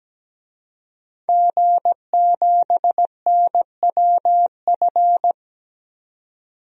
Morse code, heard as a keyed tone, about 17 wpm, 700 Hz. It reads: G7NWF
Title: DXCC-100 Call Signs - 5 Characters - 17wpm